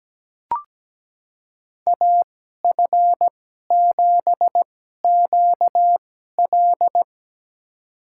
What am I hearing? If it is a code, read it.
Morse code, heard as a keyed tone, about 17 wpm, spaced as for 12 wpm, 700 Hz. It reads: AF7QL
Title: USA Call Signs - 2x Character Spacing - 17wpm